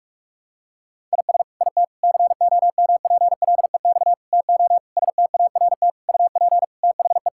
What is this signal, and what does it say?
Morse code, heard as a keyed tone, about 46 wpm, 700 Hz. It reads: IS IT COMPLEX TO START UP THE
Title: Sentences from Top 1000 Words - Review Part 6 - 45wpm